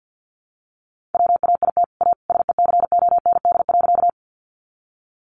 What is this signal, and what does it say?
Morse code, heard as a keyed tone, about 50 wpm, 700 Hz. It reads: WAIT A SECOND.
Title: Sentences from Top 500 Words 50wpm